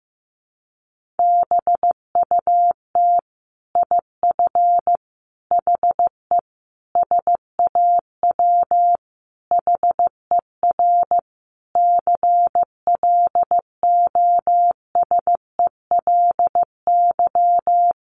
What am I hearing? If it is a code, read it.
Morse code, heard as a keyed tone, about 15 wpm, 700 Hz. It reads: BUT IF HE SAW HER CLOSELY
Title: Sentences from Top 1400 Words 15wpm